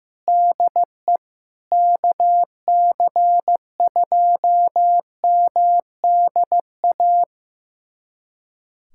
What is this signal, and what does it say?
Morse code, heard as a keyed tone, about 15 wpm, 700 Hz. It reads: DE KC2MDA
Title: USA Callsign DE Callsign - 6 Characters - 15wpm